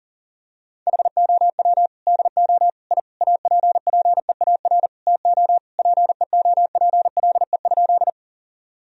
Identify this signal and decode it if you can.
Morse code, heard as a keyed tone, about 40 wpm, 700 Hz. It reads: HOW DO I APPEAR TO PEOPLE?
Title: Sentences from Top 1000 Words - Review Part 2 - 40wpm